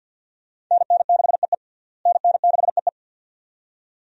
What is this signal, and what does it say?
Morse code, heard as a keyed tone, about 48 wpm, 700 Hz. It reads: NN6EE NN6EE
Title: USA Call Signs - 5 Characters - Repeated 2x - 50wpm